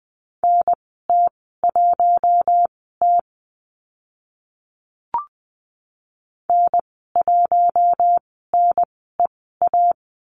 Morse code, heard as a keyed tone, about 20 wpm, spaced as for 14 wpm, 700 Hz. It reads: NT1T N1NEA
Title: 042 Lesson 8g - Number 1 - Call Signs 20wpm